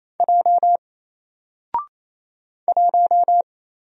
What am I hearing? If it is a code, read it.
Morse code, heard as a keyed tone, about 27 wpm, 700 Hz. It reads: J 1
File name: Single-Letter-Number-28wpm